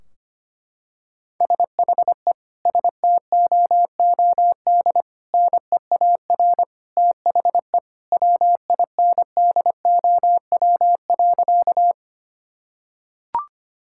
Morse code, heard as a keyed tone, about 25 wpm, 700 Hz. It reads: SHE STOOD NEAR THE WINDOW.
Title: Sentences from Top 1600 Words 25wpm